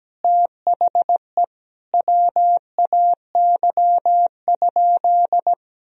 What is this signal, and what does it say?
Morse code, heard as a keyed tone, about 17 wpm, 700 Hz. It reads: THE WAY?
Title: Sentences from Top 1000 Words - Review Part 1 - 17wpm